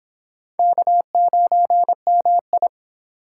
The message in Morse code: K9MI

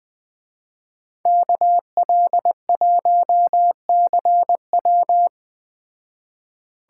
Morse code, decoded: KL1CW

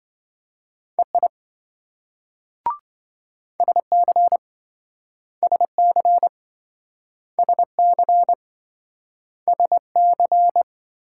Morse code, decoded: EI SC SC SC SC

